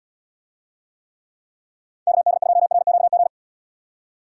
Morse code, heard as a keyed tone, about 77 wpm, 700 Hz. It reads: KD2RQG